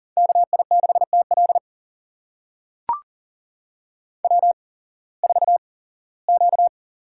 Morse code, heard as a keyed tone, about 40 wpm, spaced as for 21 wpm, 700 Hz. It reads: KI6TL W4Q
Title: USA Call Signs - 8x Character Spacing - 40wpm